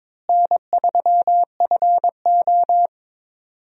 Morse code, heard as a keyed tone, about 22 wpm, 700 Hz. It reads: N3FO
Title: USA Call Signs - 4 Characters - Repeated 2x - 22wpm